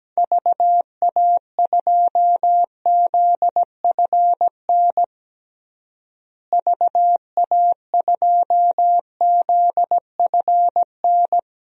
Morse code, heard as a keyed tone, about 17 wpm, 700 Hz. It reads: VA2ZFN VA2ZFN
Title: DXCC-100 Call Signs - 6 Characters - Repeated 3x - 17wpm